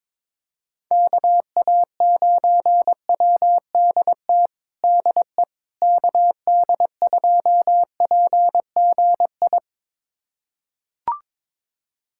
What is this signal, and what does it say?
Morse code, heard as a keyed tone, about 21 wpm, 700 Hz. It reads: KA9WDT DE KD2PGI
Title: USA Callsign DE Callsign - 6 Characters - 22wpm